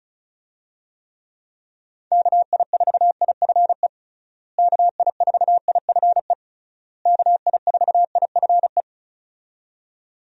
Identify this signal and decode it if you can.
Morse code, heard as a keyed tone, about 34 wpm, 700 Hz. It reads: KI4IFE KI4IFE KI4IFE
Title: USA Call Signs - 6 Characters - Repeated 3x - 35wpm